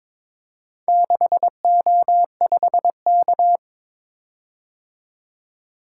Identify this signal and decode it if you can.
Morse code, heard as a keyed tone, about 22 wpm, 700 Hz. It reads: 6O5K
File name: practice / 22wpm / Four Letter-Number Combinations - 22wpm